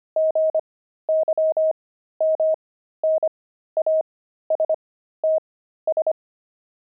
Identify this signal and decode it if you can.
Morse code, heard as a keyed tone, about 25 wpm, spaced as for 13 wpm, 630 Hz. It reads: GYMNASTS
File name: pangram-fs-13-cs-25-frogs